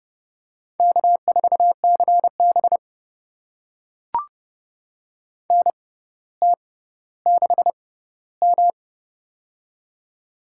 Morse code, heard as a keyed tone, about 31 wpm, spaced as for 16 wpm, 700 Hz. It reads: K4CB NT6M
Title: USA Call Signs - 4 Characters - 6x Character Spacing - 30wpm